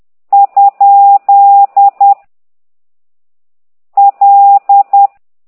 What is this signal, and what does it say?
Morse code, heard as a keyed tone, about 10 wpm, 800 Hz. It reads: ? L
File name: ct2idw / exerc9